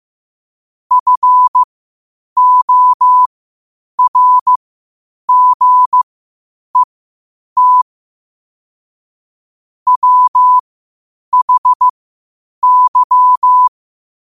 Morse code, heard as a keyed tone, about 15 wpm, spaced as for 8 wpm, 1000 Hz. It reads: FORGET WHY
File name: qotdfiles / 8wpm